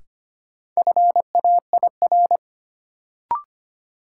Morse code, heard as a keyed tone, about 25 wpm, 700 Hz. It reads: FAIR
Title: Sentences from Top 1300 Words - New Word Drill Alt - 25wpm